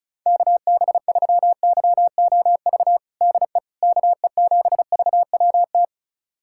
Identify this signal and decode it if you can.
Morse code, heard as a keyed tone, about 34 wpm, 700 Hz. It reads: KB3YOV DE KE7VWT
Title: USA Callsign DE Callsign - 6 Characters - 35wpm